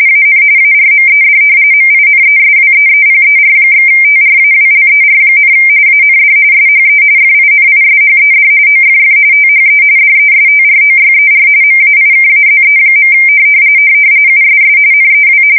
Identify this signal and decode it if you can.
RTTY, about 45.45 baud. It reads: AN ARMED MINESWEEPER/TRAWLER, AND SHE FINISHED OFF THE DESTROYER MAIKAZE. USS NEW
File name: fsk170 / battleshipnj